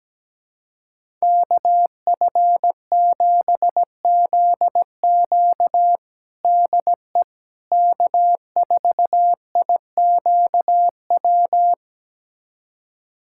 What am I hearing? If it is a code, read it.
Morse code, heard as a keyed tone, about 17 wpm, 700 Hz. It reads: KF7ZQ DE K4IQW
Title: USA Callsign DE Callsign - 5 Characters - 17wpm